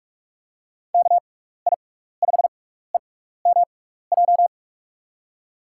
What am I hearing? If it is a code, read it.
Morse code, heard as a keyed tone, about 44 wpm, spaced as for 15 wpm, 700 Hz. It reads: KI5EMJ
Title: USA Call Signs - 6 Characters - 6x Character Spacing - 45wpm